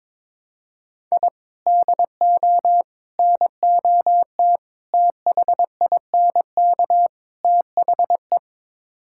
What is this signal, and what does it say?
Morse code, heard as a keyed tone, about 22 wpm, 700 Hz. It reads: I DO NOT THINK THE
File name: practice / 22wpm / Build A Phrase - 5-Word Phrases - 22wpm